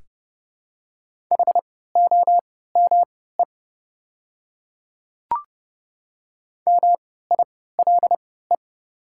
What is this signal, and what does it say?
Morse code, heard as a keyed tone, about 30 wpm, spaced as for 17 wpm, 700 Hz. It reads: HOME MILE
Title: 3x Character Spacing - Top 300 Words - 30wpm